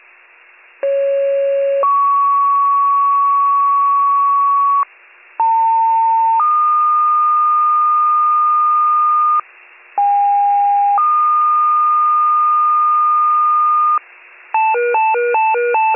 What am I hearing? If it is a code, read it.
Two-tone page, heard as a one-second tone, then a three-second tone, about 550 then 1120 Hz, about 900 then 1220 Hz, about 810 then 1220 Hz. It is heard from 0.8 to 14.0 s.